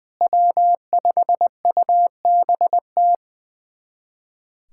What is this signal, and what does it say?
Morse code, heard as a keyed tone, about 21 wpm, 700 Hz. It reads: W5UBT
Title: USA Callsign DE Callsign - 5 Characters - 20wpm